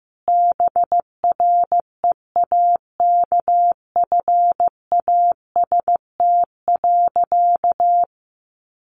Morse code, heard as a keyed tone, about 15 wpm, 700 Hz. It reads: BREAKFAST.
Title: Sentences from Top 1500 Words 15wpm